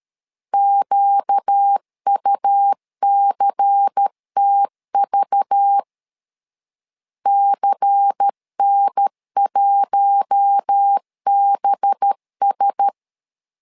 Morse code, heard as a keyed tone, about 13 wpm, 800 Hz. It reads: QUCTV CN1BS
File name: LEZIONI / 5 LBVC / 5 LBVC 12wpm 1955